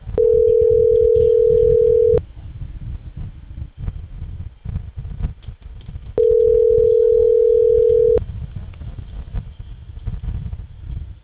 Ringback tone (North America), 440 + 480 Hz.
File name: CENTRAL CAEA / out agcaea02 - .32731